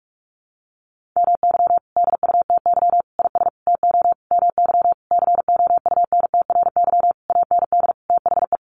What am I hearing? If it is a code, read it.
Morse code, heard as a keyed tone, about 44 wpm, 700 Hz. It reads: MY DUTY IS TO MY COUNTRY AND THE
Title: Sentences from Top 1500 Words 45wpm